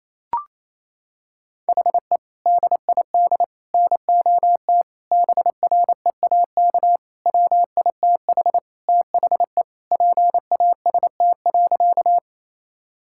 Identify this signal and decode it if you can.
Morse code, heard as a keyed tone, about 27 wpm, 700 Hz. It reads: HE DID NOT BREAK WITH THE PAST.
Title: Sentences from Top 1000 Words - Review Part 6 - 28wpm